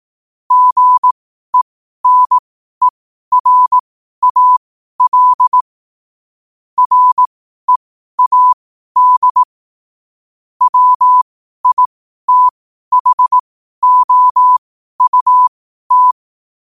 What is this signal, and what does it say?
Morse code, heard as a keyed tone, about 18 wpm, spaced as for 12 wpm, 1000 Hz. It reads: GENERAL READ WITHOUT
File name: qotdfiles / 12wpm